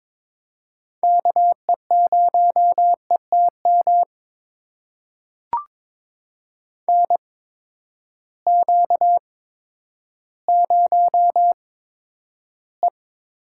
Morse code, heard as a keyed tone, about 22 wpm, spaced as for 11 wpm, 700 Hz. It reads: KE0ETM NQ0E
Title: USA Call Signs - 8x Character Spacing - 22wpm